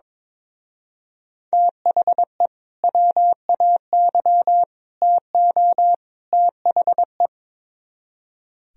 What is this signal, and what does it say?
Morse code, heard as a keyed tone, about 22 wpm, 700 Hz. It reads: THE WAY TO THE
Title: Speed-Racing - Sets of 4 Words 22wpm